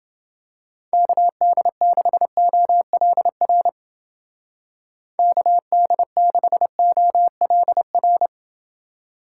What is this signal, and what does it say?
Morse code, heard as a keyed tone, about 29 wpm, 700 Hz. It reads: KD6OLR KD6OLR